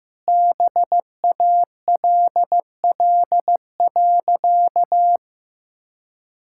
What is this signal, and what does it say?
Morse code, heard as a keyed tone, about 15 wpm, 700 Hz. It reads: BALL.
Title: Sentences from Top 1000 Words - Review Part 6 - 15wpm